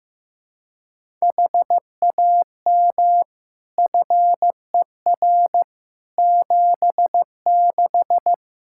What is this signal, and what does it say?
Morse code, heard as a keyed tone, about 15 wpm, 700 Hz. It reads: HAM FER 76
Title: K1USN SST QSO Practice 15wpm